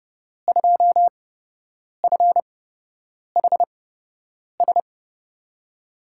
Morse code, heard as a keyed tone, about 30 wpm, spaced as for 8 wpm, 700 Hz. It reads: 2FHS
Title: USA Call Signs - 5 Characters - 8x Character Spacing - 30wpm